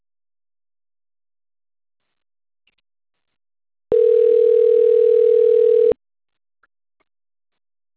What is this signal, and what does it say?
Ringback tone (North America), 440 + 480 Hz.